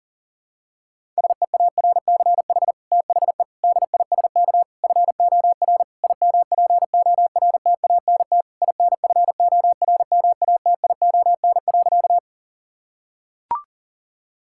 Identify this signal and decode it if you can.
Morse code, heard as a keyed tone, about 40 wpm, 690 Hz. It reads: SEARCH THE DISK FOR IMPORTANT INFORMATION.
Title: Sentences from Top 1000 Words - Review Part 6 - 40wpm